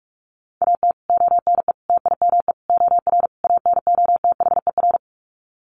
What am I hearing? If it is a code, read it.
Morse code, heard as a keyed tone, about 44 wpm, 700 Hz. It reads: AT ONE TIME OR ANOTHER
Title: Sets of 5 Words Encore 45wpm